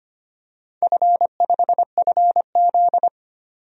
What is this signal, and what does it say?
Morse code, heard as a keyed tone, about 25 wpm, 700 Hz. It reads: F5FZ
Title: DXCC-100 Call Signs - 4 Characters - 25wpm